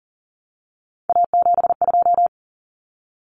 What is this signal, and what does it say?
Morse code, heard as a keyed tone, about 40 wpm, 700 Hz. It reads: A72
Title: Common Cut Numbers - 3-Digit 40wpm